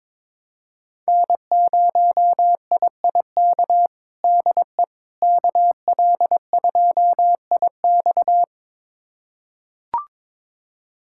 Morse code, heard as a keyed tone, about 21 wpm, 700 Hz. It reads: N0IIK DE KL2IX
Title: USA Callsign DE Callsign - 5 Characters - 22wpm